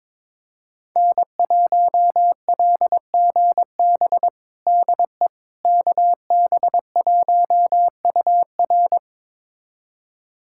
Morse code, heard as a keyed tone, about 22 wpm, 700 Hz. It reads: N1LGB DE KB1UR
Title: USA Callsign DE Callsign - 5 Characters - 22wpm